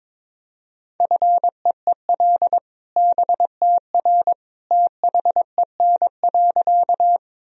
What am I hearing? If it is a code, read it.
Morse code, heard as a keyed tone, about 21 wpm, 700 Hz. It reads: FEEL BTR THEN.
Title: Sentences from Top 400 Words and Abbreviations 22wpm